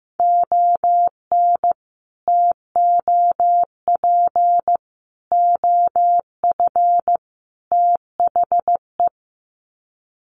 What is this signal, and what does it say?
Morse code, heard as a keyed tone, about 15 wpm, 700 Hz. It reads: ON TOP OF THE
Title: Sets of 4 Words 15wpm